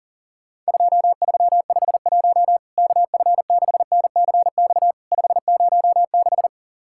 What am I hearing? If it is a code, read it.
Morse code, heard as a keyed tone, about 37 wpm, 700 Hz. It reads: 2351 KF6NCX 506
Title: Worked All Europe DX Contest - QTC Exchange - 40wpm